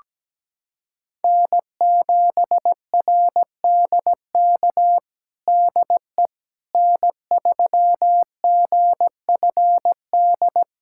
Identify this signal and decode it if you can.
Morse code, heard as a keyed tone, about 17 wpm, 700 Hz. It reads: N7RDK DE N3GFD
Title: USA Callsign DE Callsign - 5 Characters - 17wpm